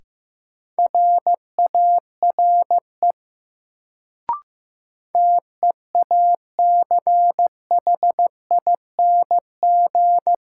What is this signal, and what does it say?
Morse code, heard as a keyed tone, about 15 wpm, 700 Hz. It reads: RARE TEACHING
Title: Sentences from Top 1100 Words - New Word Drill - 15wpm